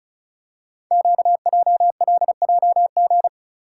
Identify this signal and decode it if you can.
Morse code, heard as a keyed tone, about 36 wpm, 690 Hz. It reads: QJLJG
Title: Code Groups - 5 Letters - 35wpm